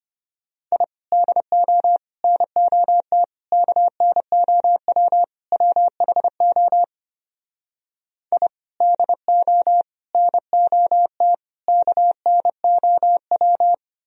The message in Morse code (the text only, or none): I DO NOT KNOW WHO I DO NOT KNOW